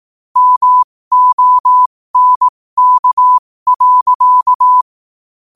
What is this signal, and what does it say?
Morse code, heard as a keyed tone, about 18 wpm, spaced as for 15 wpm, 1000 Hz. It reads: MONK.
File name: qotdfiles / 15wpm